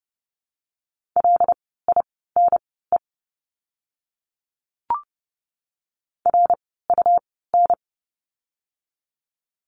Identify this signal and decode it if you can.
Morse code, heard as a keyed tone, about 30 wpm, spaced as for 17 wpm, 700 Hz. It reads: LINE RUN
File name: course / 30:10wpm / 115 Lesson 19f - Letter w - Words 30wpm